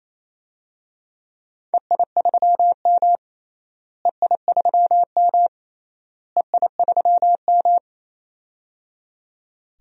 Morse code, heard as a keyed tone, about 27 wpm, 700 Hz. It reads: EI3M EI3M EI3M
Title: DXCC-100 Call Signs - 4 Characters - Repeated 3x - 28wpm